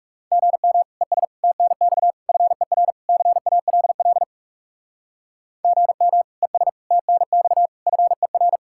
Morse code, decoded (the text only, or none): GM ES TNX FER CALL GM ES TNX FER